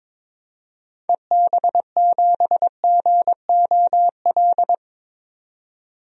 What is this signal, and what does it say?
Morse code, heard as a keyed tone, about 21 wpm, 700 Hz. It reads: EB7GOL
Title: DXCC-100 Call Signs - 6 Characters - 22wpm